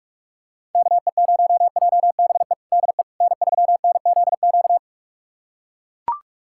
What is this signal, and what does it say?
Morse code, heard as a keyed tone, about 41 wpm, 700 Hz. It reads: KE0JBE DE N3NZQ
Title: USA Callsign DE Callsign 45wpm